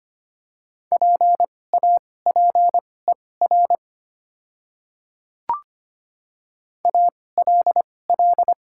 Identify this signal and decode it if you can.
Morse code, heard as a keyed tone, about 25 wpm, spaced as for 18 wpm, 700 Hz. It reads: PAPER ALL
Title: 2x Character Spacing - Top 300 Words - 25wpm